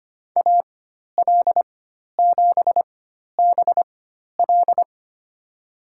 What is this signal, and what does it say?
Morse code, heard as a keyed tone, about 25 wpm, spaced as for 12 wpm, 700 Hz. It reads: AL7BL
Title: USA Call Signs - 5 Characters - 4x Character Spacing - 25wpm